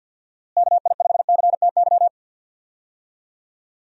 Morse code, heard as a keyed tone, about 46 wpm, 700 Hz. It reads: KI5CTY
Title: USA Call Signs - 6 Characters - Repeated 3x - 50wpm